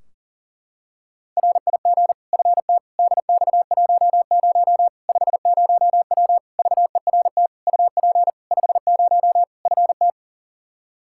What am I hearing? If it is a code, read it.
Morse code, heard as a keyed tone, about 40 wpm, 700 Hz. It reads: RIG FT DX10 50W VERT UP 50 FT
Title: Practice QSO - Exchange 2 - Info - 40wpm